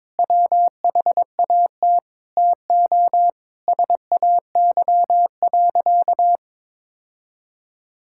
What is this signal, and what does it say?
Morse code, heard as a keyed tone, about 22 wpm, 700 Hz. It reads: WHAT TO SAY.
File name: practice / 22wpm / Sentences with 4-Word Phrases - 22wpm